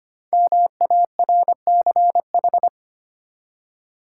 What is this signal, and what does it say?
Morse code, heard as a keyed tone, about 25 wpm, 700 Hz. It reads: MARCH
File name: 5 Letter Words - First 3 Letters Given - 25wpm